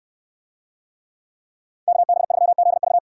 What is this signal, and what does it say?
Morse code, heard as a keyed tone, about 69 wpm, 700 Hz. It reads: KB3ZV